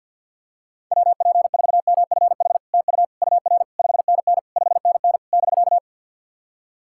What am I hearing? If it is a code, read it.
Morse code, heard as a keyed tone, about 53 wpm, 690 Hz. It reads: WP4MFH TU UR 5NN 5NN <BK>
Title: SOTA QSO Practice 50wpm